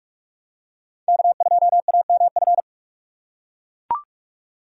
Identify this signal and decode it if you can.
Morse code, heard as a keyed tone, about 40 wpm, 700 Hz. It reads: K2AMF